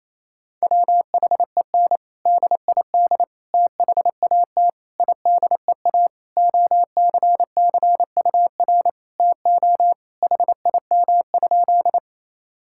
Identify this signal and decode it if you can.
Morse code, heard as a keyed tone, about 27 wpm, 700 Hz. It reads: WHEN DID THAT IDEA OCCUR TO HIM?
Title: Sentences from Top 1000 Words - Review Part 6 - 28wpm